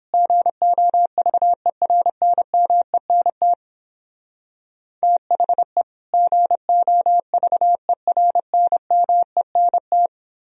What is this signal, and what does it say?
Morse code, heard as a keyed tone, about 29 wpm, 700 Hz. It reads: GOVERNMENT THE GOVERNMENT